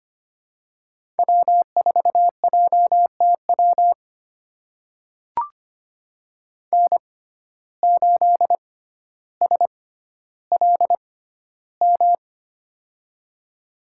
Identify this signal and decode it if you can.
Morse code, heard as a keyed tone, about 25 wpm, spaced as for 13 wpm, 700 Hz. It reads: W4JTW N8SLM
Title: USA Call Signs - 5 Characters - 6x Character Spacing - 25wpm